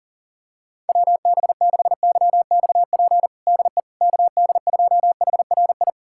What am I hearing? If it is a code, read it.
Morse code, heard as a keyed tone, about 39 wpm, 700 Hz. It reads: WB6YXP DE KD2HRI
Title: USA Callsign DE Callsign 40wpm